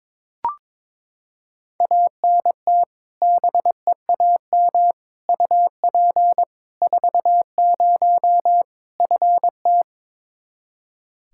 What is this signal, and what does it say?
Morse code, heard as a keyed tone, about 22 wpm, 700 Hz. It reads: ANT BEAM UP 40 FT
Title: Practice QSO - Exchange 2 - Info - 22wpm